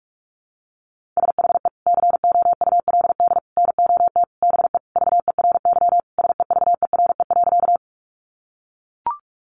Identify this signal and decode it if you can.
Morse code, heard as a keyed tone, about 44 wpm, 700 Hz. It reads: SHE COULD NOT BE VERY SEVERE.
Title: Sentences from Top 1400 Words 45wpm